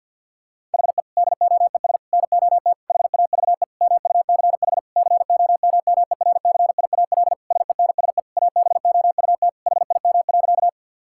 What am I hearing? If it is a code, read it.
Morse code, heard as a keyed tone, about 50 wpm, 700 Hz. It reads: HE DOES NOT HAVE MUCH COMMERCIAL SENSE ABOUT HIM.
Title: Sentences from Top 1000 Words - Review Part 5 - 50wpm